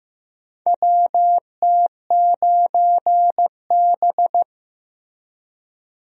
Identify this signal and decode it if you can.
Morse code, heard as a keyed tone, about 15 wpm, 700 Hz. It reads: WT9B
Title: USA Call Signs - 4 Characters - Repeated 3x - 15wpm